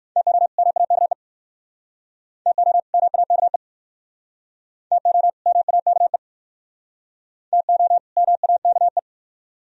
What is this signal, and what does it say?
Morse code, heard as a keyed tone, about 60 wpm, 700 Hz. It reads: TO MAKE TO MAKE TO MAKE TO MAKE